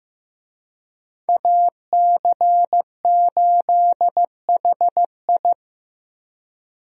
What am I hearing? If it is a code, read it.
Morse code, heard as a keyed tone, about 15 wpm, 700 Hz. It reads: AC8HI
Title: USA Call Signs - 5 Characters - 15wpm